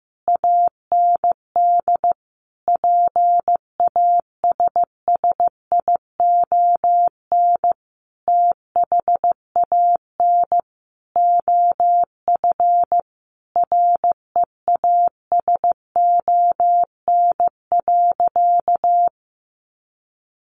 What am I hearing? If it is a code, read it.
Morse code, heard as a keyed tone, about 15 wpm, 700 Hz. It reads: AND PASSION THAN OF REASON.
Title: Sentences from Top 1300 Words 15wpm